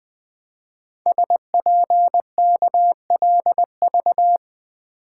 Morse code, heard as a keyed tone, about 20 wpm, 700 Hz. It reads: SPKLV
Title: Code Groups - 5 Letters - 20wpm